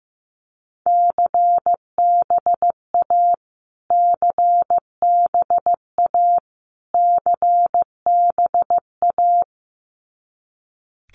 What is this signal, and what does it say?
Morse code, heard as a keyed tone, about 15 wpm, 700 Hz. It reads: CBA CBA CBA
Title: Abbreviations Repeated 3x - 15wpm